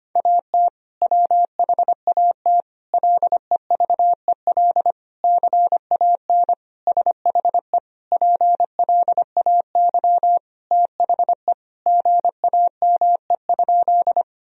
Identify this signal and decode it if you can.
Morse code, heard as a keyed tone, about 25 wpm, 700 Hz. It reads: AT WHAT LEVEL CAN SHE PLAY THE GAME?
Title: Sentences from Top 1000 Words - Review Part 4 - 25wpm